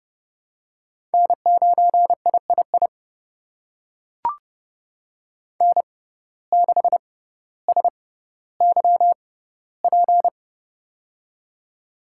Morse code, heard as a keyed tone, about 30 wpm, spaced as for 15 wpm, 700 Hz. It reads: N9III N6SYP
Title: USA Call Signs - 5 Characters - 6x Character Spacing - 30wpm